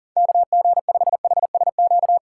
Morse code, heard as a keyed tone, about 40 wpm, 700 Hz. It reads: KG5HSQ
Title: USA Call Signs - 6 Characters - 4x Character Spacing - 40wpm